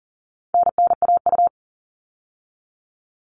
Morse code, heard as a keyed tone, about 40 wpm, 700 Hz. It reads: NNAU
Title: All Cut Numbers - 4-Digit 40wpm